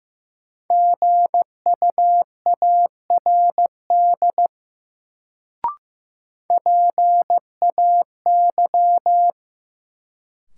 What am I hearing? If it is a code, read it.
Morse code, heard as a keyed tone, about 15 wpm, 700 Hz. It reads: GUARD PAY
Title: Sentences from Top 700 Words - New Word Drill - 15wpm